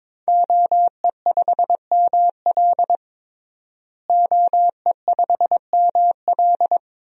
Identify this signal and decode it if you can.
Morse code, heard as a keyed tone, about 22 wpm, 700 Hz. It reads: OE5ML OE5ML
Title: DXCC-100 Call Signs - 5 Characters - Repeated 3x - 22wpm